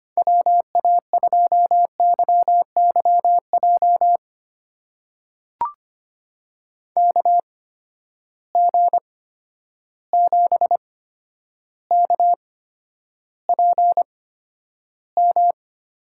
Morse code, heard as a keyed tone, about 25 wpm, spaced as for 11 wpm, 700 Hz. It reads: WA2YYJ KG7KPM
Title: USA Call Signs - 6 Characters - 8x Character Spacing - 25wpm